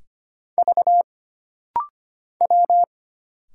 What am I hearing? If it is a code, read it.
Morse code, heard as a keyed tone, about 25 wpm, 700 Hz. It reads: V W
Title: Single Letters - Rapid-Fire 25wpm